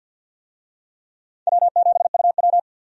Morse code, heard as a keyed tone, about 52 wpm, 700 Hz. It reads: W7UW
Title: USA Call Signs - 4 Characters - 50wpm